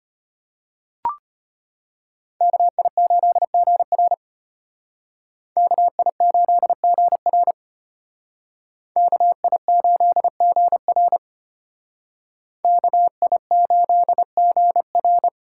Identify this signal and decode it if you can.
Morse code, heard as a keyed tone, about 32 wpm, 700 Hz. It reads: KI8GR KI8GR KI8GR KI8GR